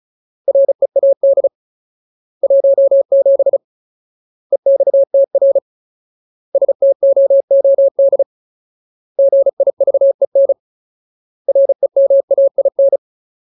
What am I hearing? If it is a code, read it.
Morse code, heard as a keyed tone, about 35 wpm, 550 Hz. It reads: READ 17 EXTR STOOD GIVEN REMAIN
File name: Ultimate Word List / 35WPM Ultimate